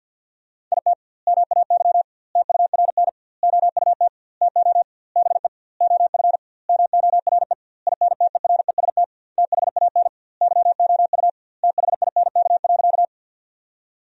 Morse code, heard as a keyed tone, about 50 wpm, 700 Hz. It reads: IT MAY TURN OUT TO BE OF MORE INTEREST THAN YOU THINK.
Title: 2x Word-Spacing - Sentences from Top 400 Words - 50wpm